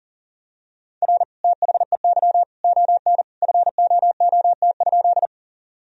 Morse code, heard as a keyed tone, about 40 wpm, 700 Hz. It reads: R THEY ON FOOT?
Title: Sentences from Top 500 Words and Abbreviations 40wpm